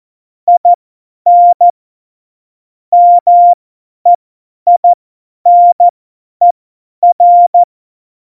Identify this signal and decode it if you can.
Morse code, heard as a keyed tone, about 14 wpm, spaced as for 10 wpm, 700 Hz. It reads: IN MEINER
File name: Morsekurs / Witze / Witz 09. April 2019, 50 F70